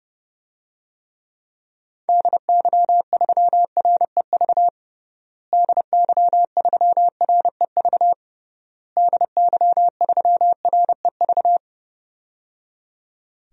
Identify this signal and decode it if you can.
Morse code, heard as a keyed tone, about 30 wpm, 700 Hz. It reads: DY3REV DY3REV DY3REV
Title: DXCC-100 Call Signs - 6 Characters - Repeated 3x - 30wpm